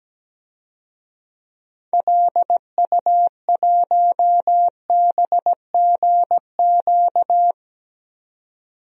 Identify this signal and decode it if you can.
Morse code, heard as a keyed tone, about 17 wpm, 700 Hz. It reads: LU1BGQ
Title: DXCC-100 Call Signs - 6 Characters - 17wpm